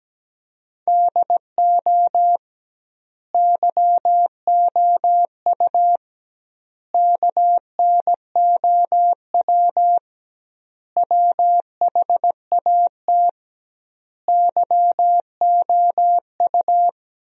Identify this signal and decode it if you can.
Morse code, heard as a keyed tone, about 17 wpm, 700 Hz. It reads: DO YOU KNOW WHAT YOU
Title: 2x Word-Spacing - Sentences from Top 100 Words - 17wpm